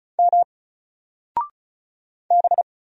Morse code, heard as a keyed tone, about 33 wpm, 700 Hz. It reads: M B
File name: Single-Letter-Number-35wpm